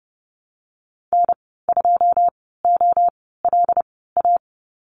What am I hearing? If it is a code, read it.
Morse code, heard as a keyed tone, about 30 wpm, spaced as for 17 wpm, 700 Hz. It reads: N2OLA
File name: Lesson 15g - Number 5 - Call Signs 30wpm